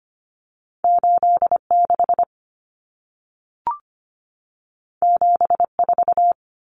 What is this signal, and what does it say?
Morse code, heard as a keyed tone, about 25 wpm, 700 Hz. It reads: 86 74
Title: Common Cut Numbers - 2-Digit 25wpm